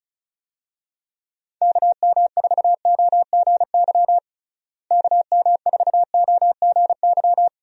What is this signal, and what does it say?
Morse code, heard as a keyed tone, about 36 wpm, 700 Hz. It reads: KM4OGY KM4OGY
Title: USA Call Signs - 6 Characters - Repeated 3x - 35wpm